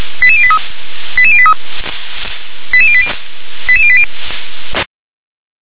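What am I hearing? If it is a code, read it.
Five-tone selective call (ZVEI), heard as quick runs of five tones, tones of about 70 ms. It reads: 88083 88083 88080 88080
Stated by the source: ZVEI five-tone selective call received by SDR.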